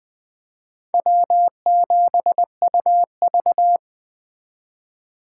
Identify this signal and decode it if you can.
Morse code, heard as a keyed tone, about 24 wpm, 700 Hz. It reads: W7UV
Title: Speed Racing - USA 1x2 Call Signs 20wpm